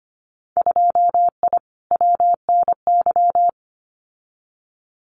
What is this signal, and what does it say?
Morse code, heard as a keyed tone, about 25 wpm, 700 Hz. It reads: 2I WNY
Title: Winter Field Day Exchanges 25wpm